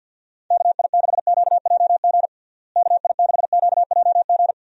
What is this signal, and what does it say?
Morse code, heard as a keyed tone, about 46 wpm, 700 Hz. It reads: KI6QJG KI6QJG
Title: USA Call Signs - 6 Characters - Repeated 2x - 50wpm